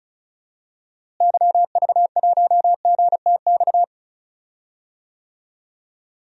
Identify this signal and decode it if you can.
Morse code, heard as a keyed tone, about 34 wpm, 700 Hz. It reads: YV1GTX
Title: DXCC-100 Call Signs - 6 Characters - Repeated 2x - 35wpm